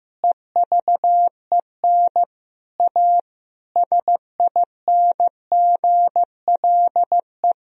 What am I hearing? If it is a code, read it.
Morse code, heard as a keyed tone, about 16 wpm, 700 Hz. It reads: EVEN A SINGLE
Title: Sentences from Top 1000 Words - Review Part 4 - 15wpm